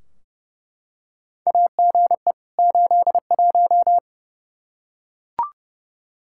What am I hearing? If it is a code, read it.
Morse code, heard as a keyed tone, about 30 wpm, 700 Hz. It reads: AGE 81
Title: Practice QSO - Exchange 3 - More Info - 30wpm